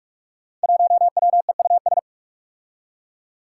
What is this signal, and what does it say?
Morse code, heard as a keyed tone, about 44 wpm, 700 Hz. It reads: 1WEUS
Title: Five Challenging Letter-Number Combinations 45wpm